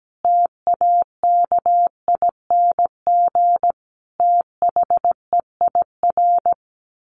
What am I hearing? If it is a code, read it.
Morse code, heard as a keyed tone, about 17 wpm, 700 Hz. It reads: TAKING THEIR
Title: A Sound Of Thunder 17wpm